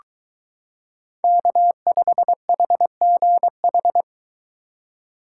Morse code, heard as a keyed tone, about 23 wpm, 700 Hz. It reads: K5HGH